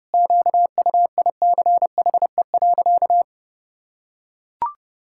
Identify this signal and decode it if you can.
Morse code, heard as a keyed tone, about 30 wpm, 700 Hz. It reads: QUICHE.
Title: Pangram Challenge v2 - 30wpm